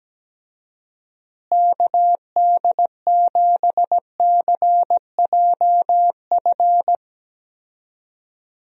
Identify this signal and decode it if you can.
Morse code, heard as a keyed tone, about 17 wpm, 700 Hz. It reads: KD7CJF
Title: USA Call Signs - 6 Characters - Repeated 3x - 17wpm